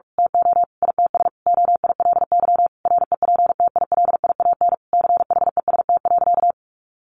Morse code, heard as a keyed tone, about 44 wpm, 700 Hz. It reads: TO ITS OILY REPTILIAN CHEST.
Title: A Sound Of Thunder 45wpm